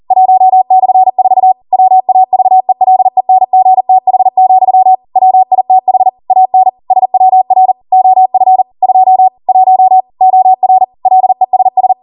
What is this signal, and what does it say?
Morse code, heard as a keyed tone, about 40 wpm, 750 Hz. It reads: 1/4 WAVELENGTH, WITH AN SWR OF 2 1 OR LESS